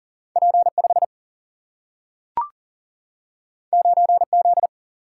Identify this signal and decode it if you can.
Morse code, heard as a keyed tone, about 40 wpm, 700 Hz. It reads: P5 9Z
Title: Single-Letter Single-Number 40wpm